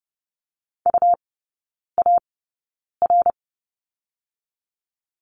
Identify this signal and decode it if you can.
Morse code, heard as a keyed tone, about 30 wpm, spaced as for 9 wpm, 690 Hz. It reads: UAR
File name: Lesson 17e - Letter u - Three Characters 30wpm